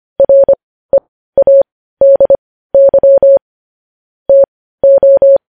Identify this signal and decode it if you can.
Morse code, heard as a keyed tone, about 25 wpm, spaced as for 15 wpm, 550 Hz. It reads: READY TO
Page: News Headlines in Morse Code at 15 WPM